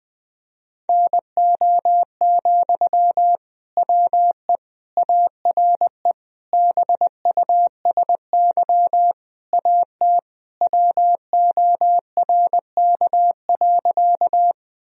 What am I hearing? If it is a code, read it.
Morse code, heard as a keyed tone, about 21 wpm, 700 Hz. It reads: NO, WE ARE BUSY AT WORK.
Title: Sentences from Top 1000 Words - Review Part 3 - 20wpm